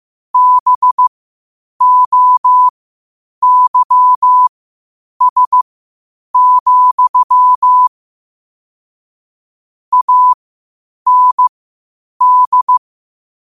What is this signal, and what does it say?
Morse code, heard as a keyed tone, about 15 wpm, spaced as for 8 wpm, 1000 Hz. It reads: BOYS, AND
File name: qotdfiles / 8wpm